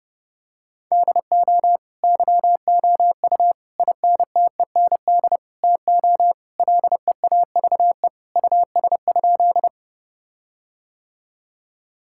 Morse code, heard as a keyed tone, about 30 wpm, 700 Hz. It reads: DO YOU INTEND TO LEAVE US?
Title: Sentences from Top 1600 Words 30wpm